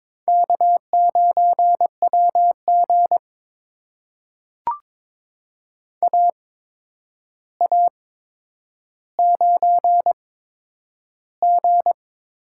Morse code, heard as a keyed tone, about 22 wpm, spaced as for 9 wpm, 700 Hz. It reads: K9WG AA9G
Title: USA Call Signs - 4 Characters - 8x Character Spacing - 22wpm